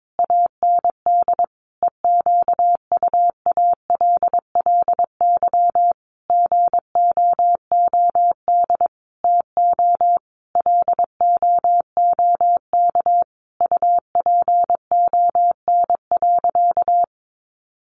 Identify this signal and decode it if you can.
Morse code, heard as a keyed tone, about 22 wpm, 700 Hz. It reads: AND EQUALLY GOOD TO LOOK UPON.
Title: Sentences from Top 1400 Words 22wpm